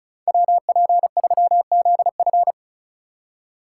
Morse code, heard as a keyed tone, about 34 wpm, 700 Hz. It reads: WP3ZF
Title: USA Call Signs - 5 Characters - 4x Character Spacing - 35wpm